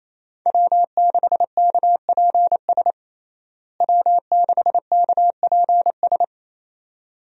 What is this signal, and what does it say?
Morse code, heard as a keyed tone, about 28 wpm, 700 Hz. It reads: W6KPS W6KPS
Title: USA Call Signs - 5 Characters - Repeated 2x - 28wpm